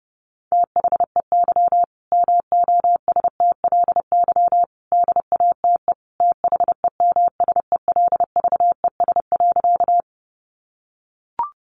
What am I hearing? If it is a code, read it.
Morse code, heard as a keyed tone, about 30 wpm, 700 Hz. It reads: THEY MOSTLY DATE THEMSELVES.
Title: Sentences from Top 1300 Words 30wpm